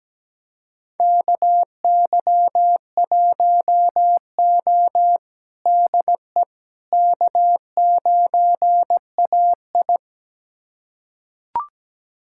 Morse code, heard as a keyed tone, about 17 wpm, 700 Hz. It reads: KY1O DE K9AI
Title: USA Callsign DE Callsign - 4 Characters - 17wpm